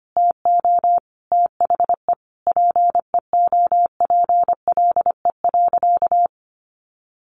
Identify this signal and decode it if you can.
Morse code, heard as a keyed tone, about 25 wpm, 700 Hz. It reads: TO THE PEOPLE.
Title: Sentences from Top 1500 Words 25wpm